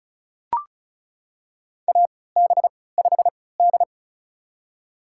Morse code, heard as a keyed tone, about 34 wpm, spaced as for 20 wpm, 700 Hz. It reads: AB5D
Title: USA Call Signs - 3x Character Spacing - 35wpm